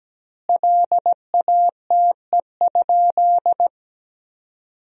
Morse code, heard as a keyed tone, about 17 wpm, 700 Hz. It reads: LATE?
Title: 2x Word-Spacing - Sentences from Top 300 Words - 17wpm